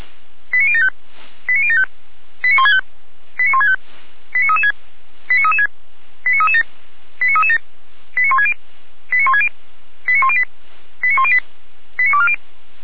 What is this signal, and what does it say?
Five-tone selective call (ZVEI), heard as quick runs of five tones, tones of about 70 ms. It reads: 89075 89075 89165 89165 89336 89336 89337 89337 89160 89160 89108 89108 89240
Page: /455/SDR/ZVEI/